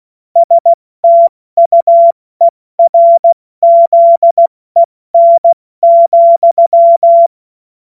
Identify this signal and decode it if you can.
Morse code, heard as a keyed tone, about 16 wpm, spaced as for 14 wpm, 680 Hz. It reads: STUERZEN,
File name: Witz-15-Juli-2025-70-F80